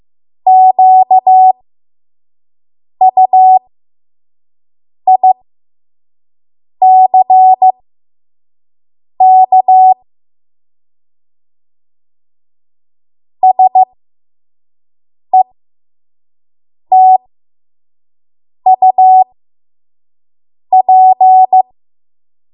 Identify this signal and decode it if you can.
Morse code, heard as a keyed tone, about 15 wpm, spaced as for 5 wpm, 750 Hz. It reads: QUICK SETUP
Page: 5 WPM Code Archive